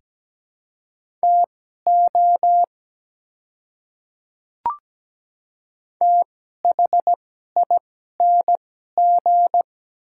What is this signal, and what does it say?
Morse code, heard as a keyed tone, about 17 wpm, spaced as for 12 wpm, 700 Hz. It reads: TO THING
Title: 2x Character Spacing - Top 100 Words - 17wpm